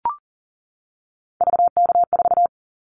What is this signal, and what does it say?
Morse code, heard as a keyed tone, about 38 wpm, 700 Hz. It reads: VK4
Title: Callsign Prefixes for Top 50 DXCC - 40wpm